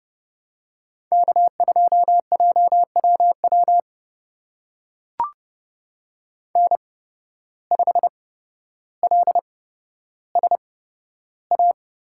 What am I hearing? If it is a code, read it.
Morse code, heard as a keyed tone, about 30 wpm, spaced as for 13 wpm, 700 Hz. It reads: K2JWW N5LSA
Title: USA Call Signs - 5 Characters - 8x Character Spacing - 30wpm